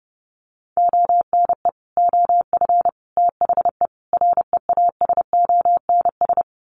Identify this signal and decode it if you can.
Morse code, heard as a keyed tone, about 30 wpm, 700 Hz. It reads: ONE OF THE REASONS
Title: Sets of 4 Words 30wpm